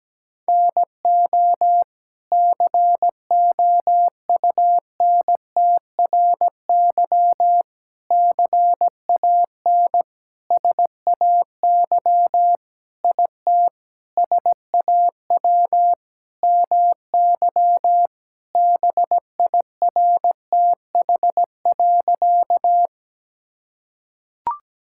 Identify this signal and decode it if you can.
Morse code, heard as a keyed tone, about 17 wpm, 700 Hz. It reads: NO COUNTRY CAN SAY IT SAW MY BIRTH.
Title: Sentences from Top 1000 Words - Review Part 5 - 17wpm